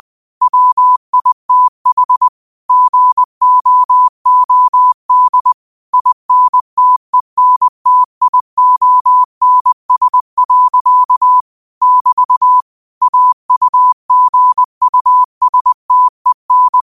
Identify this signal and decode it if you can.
Morse code, heard as a keyed tone, about 20 wpm, 1000 Hz. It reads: WITH GOOD INTENTIONS. <BT> AUGUSTEN
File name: qotdfiles / 20wpm